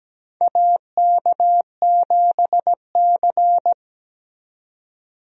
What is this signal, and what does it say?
Morse code, heard as a keyed tone, about 17 wpm, 700 Hz. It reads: AK7C
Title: USA Call Signs - 4 Characters - 17wpm